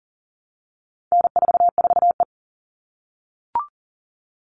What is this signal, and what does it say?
Morse code, heard as a keyed tone, about 40 wpm, 700 Hz. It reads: N44E
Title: All Cut Numbers - 4-Digit 40wpm